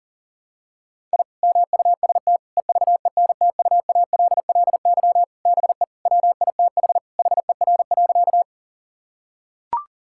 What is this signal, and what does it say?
Morse code, heard as a keyed tone, about 37 wpm, 700 Hz. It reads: I MUST EVENTUALLY BE WITH HER.
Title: Sentences from Top 1000 Words - Review Part 6 - 40wpm